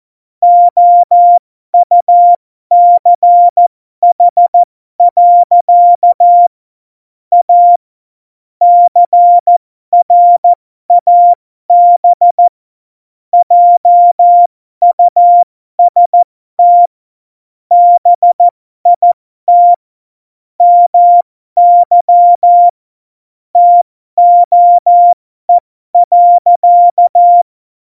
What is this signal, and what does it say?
Morse code, heard as a keyed tone, about 14 wpm, spaced as for 12 wpm, 700 Hz. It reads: OUCH. A CRAB JUST BIT MY TOE.
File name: Morsekurs / Witze